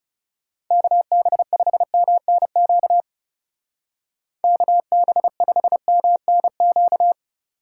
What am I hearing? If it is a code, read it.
Morse code, heard as a keyed tone, about 36 wpm, 700 Hz. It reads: KB5MNQ KB5MNQ
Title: Speed Racing - USA 2x3 Call Signs 30wpm